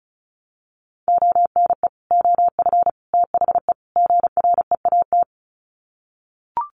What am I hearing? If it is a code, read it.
Morse code, heard as a keyed tone, about 34 wpm, 700 Hz. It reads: ONE OF THE GREAT
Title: Sets of 4 Words 35wpm